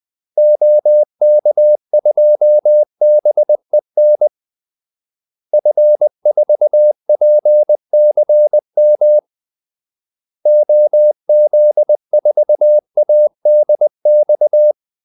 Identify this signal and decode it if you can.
Morse code, heard as a keyed tone, about 20 wpm, 600 Hz. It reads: OK2BEN F4PCM OZ4ADX
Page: Kaldesignaler 20 wpm